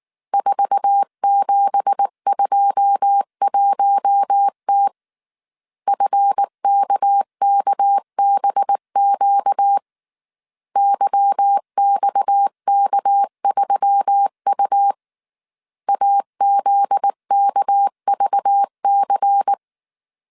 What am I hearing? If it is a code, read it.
Morse code, heard as a keyed tone, about 19 wpm, 800 Hz. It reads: 4721T FKKBQ YXK3U AZKVC